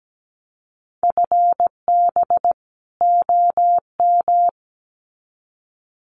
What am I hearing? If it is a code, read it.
Morse code, heard as a keyed tone, about 17 wpm, 700 Hz. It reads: FB OM
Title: Common QSO Phrases 17wpm